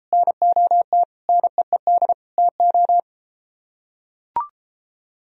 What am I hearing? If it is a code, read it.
Morse code, heard as a keyed tone, about 32 wpm, 700 Hz. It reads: NOT NEED TO